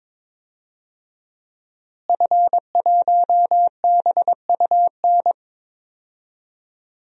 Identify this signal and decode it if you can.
Morse code, heard as a keyed tone, about 22 wpm, 700 Hz. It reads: F1BUN
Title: DXCC-100 Call Signs - 5 Characters - Repeated 3x - 22wpm